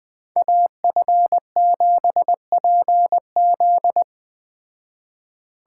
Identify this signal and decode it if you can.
Morse code, heard as a keyed tone, about 20 wpm, 700 Hz. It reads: AF7PZ